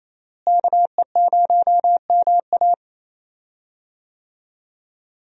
Morse code, heard as a keyed tone, about 28 wpm, 700 Hz. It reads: KE0MA
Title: USA Call Signs - 5 Characters - Repeated 3x - 28wpm